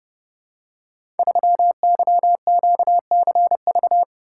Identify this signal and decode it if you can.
Morse code, heard as a keyed tone, about 29 wpm, 700 Hz. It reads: 3YQCV
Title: Code Groups - 5 Letter:Numbers - 30wpm